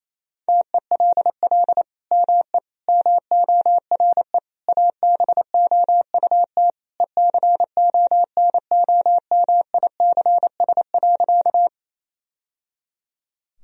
Morse code, heard as a keyed tone, about 28 wpm, 700 Hz. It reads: TELL ME MORE ABOUT ECONOMICS.
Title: Sentences from Top 1000 Words - Review Part 3 - 28wpm